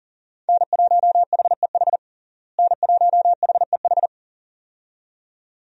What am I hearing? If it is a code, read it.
Morse code, heard as a keyed tone, about 37 wpm, 700 Hz. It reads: N1HEH N1HEH
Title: USA Call Signs - 5 Characters - Repeated 2x - 40wpm